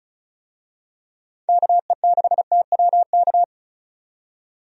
Morse code, heard as a keyed tone, about 36 wpm, 700 Hz. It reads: KE6TWK
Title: USA Call Signs - 6 Characters - 35wpm